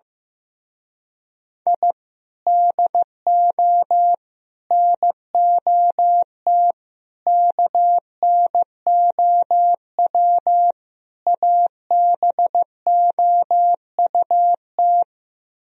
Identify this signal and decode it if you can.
Morse code, heard as a keyed tone, about 15 wpm, 700 Hz. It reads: I DO NOT KNOW ABOUT
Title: Speed-Racing - Sets of 5 Words 15wpm